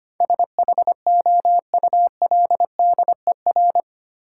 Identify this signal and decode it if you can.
Morse code, heard as a keyed tone, about 25 wpm, 700 Hz. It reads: SHOULDER
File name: Eight Letter Words - 25wpm